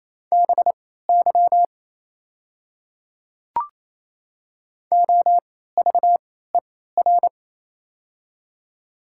Morse code, heard as a keyed tone, about 28 wpm, spaced as for 16 wpm, 700 Hz. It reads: BY OVER
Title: 3x Character Spacing - Top 200 Words - 28wpm